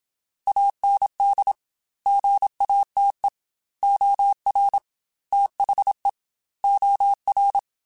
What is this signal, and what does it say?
Morse code, heard as a keyed tone, about 27 wpm, 800 Hz. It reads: AND GATE OR THE OR
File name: Technical Subjects / Technical Subjects-mp3s / AND Gate @25